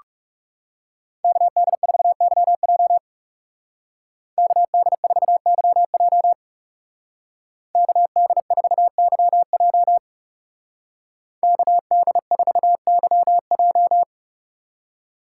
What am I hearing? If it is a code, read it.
Morse code, heard as a keyed tone, about 37 wpm, 700 Hz. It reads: KD4YJ KD4YJ KD4YJ KD4YJ
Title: Speed Racing - USA 2x2 Call Signs 30wpm